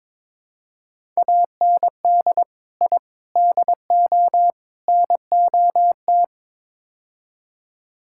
Morse code, heard as a keyed tone, about 21 wpm, 700 Hz. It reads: AND I DO NOT
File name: practice / 22wpm / Build A Phrase - 5-Word Phrases - 22wpm